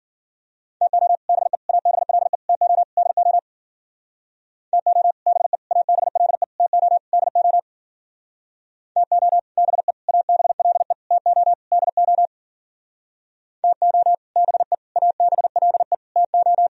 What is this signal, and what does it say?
Morse code, heard as a keyed tone, about 48 wpm, 700 Hz. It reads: TO BE ABLE TO DO TO BE ABLE TO DO TO BE ABLE TO DO TO BE ABLE TO
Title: Speed-Racing - Sets of 5 Words 40wpm